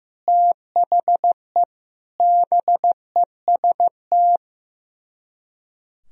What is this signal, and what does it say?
Morse code, heard as a keyed tone, about 15 wpm, 700 Hz. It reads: THE BEST
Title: Speed-Racing - Sets of 4 Words 15wpm